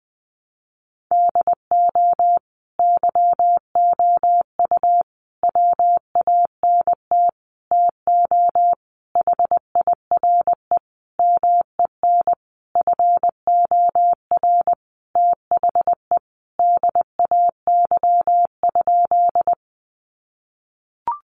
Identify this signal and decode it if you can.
Morse code, heard as a keyed tone, about 20 wpm, 700 Hz. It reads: DO YOU WANT TO HIRE MEN FOR THE DAY?
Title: Sentences from Top 1400 Words 20wpm